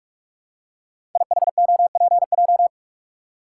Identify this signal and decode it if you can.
Morse code, heard as a keyed tone, about 44 wpm, 690 Hz. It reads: IHOPJ
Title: Code Groups - 5 Letters - 45wpm